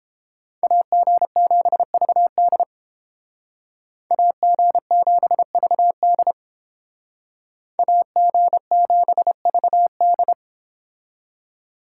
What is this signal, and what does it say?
Morse code, heard as a keyed tone, about 32 wpm, 700 Hz. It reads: AG7VD AG7VD AG7VD